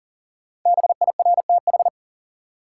Morse code, heard as a keyed tone, about 40 wpm, 700 Hz. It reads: BIRTH
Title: Sentences from Top 800 Words - New Word Drill - 40wpm